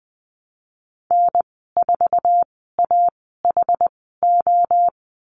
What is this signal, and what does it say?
Morse code, heard as a keyed tone, about 20 wpm, spaced as for 15 wpm, 700 Hz. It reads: N4AHO
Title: 069 Lesson 12g - Letter d - Call Signs 20wpm